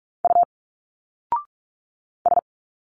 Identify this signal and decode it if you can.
Morse code, heard as a keyed tone, about 46 wpm, 700 Hz. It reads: U S
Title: Single Letters 45wpm